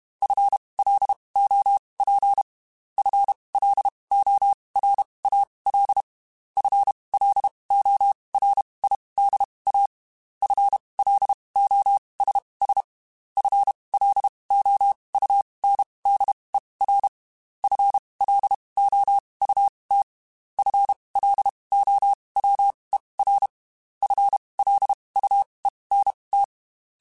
Morse code, heard as a keyed tone, about 32 wpm, spaced as for 23 wpm, 800 Hz. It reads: FLOP FLORAL FLORIDA FLOSS FLOUNDER FLOUT FLOWER FLUENT
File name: Spaced / FL Runs sp mp3s / FL Runs sp @30